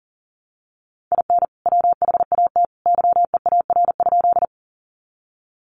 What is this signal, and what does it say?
Morse code, heard as a keyed tone, about 40 wpm, 700 Hz. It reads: IN WHAT YEAR?
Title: Sentences from Top 200 Words 40wpm